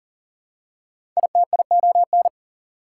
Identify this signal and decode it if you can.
Morse code, heard as a keyed tone, about 40 wpm, 700 Hz. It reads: ITION
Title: Prefixes and Suffixes 40wpm